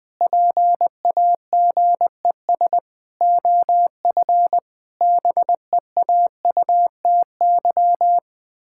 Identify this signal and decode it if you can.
Morse code, heard as a keyed tone, about 21 wpm, 700 Hz. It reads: PAGES OF BEAUTY
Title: Pangram Challenge v2 - 20wpm